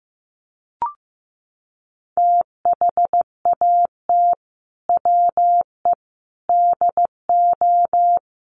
Morse code, heard as a keyed tone, about 15 wpm, 700 Hz. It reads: THAT WE DO
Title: Sets of 5 Words Encore 15wpm